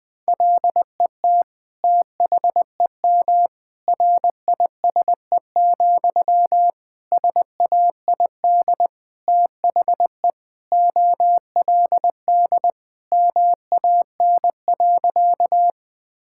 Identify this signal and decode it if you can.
Morse code, heard as a keyed tone, about 20 wpm, 700 Hz. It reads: LET THEM RISE, SAID THE OLD MAN.
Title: Sentences from Top 1000 Words - Review Part 6 - 20wpm